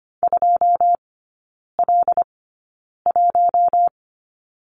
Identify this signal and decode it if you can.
Morse code, heard as a keyed tone, about 25 wpm, spaced as for 9 wpm, 700 Hz. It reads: 2L1
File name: Lesson 17e - Letter u - Three Characters 25wpm